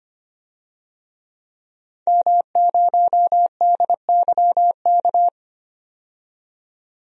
Morse code, heard as a keyed tone, about 25 wpm, 700 Hz. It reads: M0DYK
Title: DXCC-100 Call Signs - 5 Characters - 25wpm